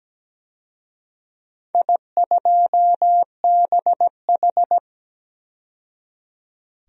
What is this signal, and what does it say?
Morse code, heard as a keyed tone, about 17 wpm, 700 Hz. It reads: I2BH
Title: DXCC-100 Call Signs - 4 Characters - 17wpm